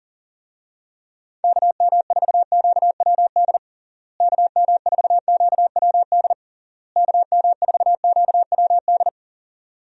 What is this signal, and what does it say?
Morse code, heard as a keyed tone, about 40 wpm, 700 Hz. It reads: KM4QWD KM4QWD KM4QWD
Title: USA Call Signs - 6 Characters - Repeated 3x - 40wpm